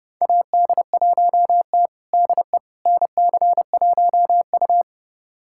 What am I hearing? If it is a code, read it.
Morse code, heard as a keyed tone, about 30 wpm, 700 Hz. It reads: AD1T DE NC1U
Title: USA Callsign DE Callsign - 4 Characters - 30wpm